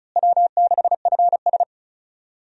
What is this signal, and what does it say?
Morse code, heard as a keyed tone, about 34 wpm, 700 Hz. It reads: W6FS
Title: USA Call Signs - 4 Characters - Repeated 2x - 35wpm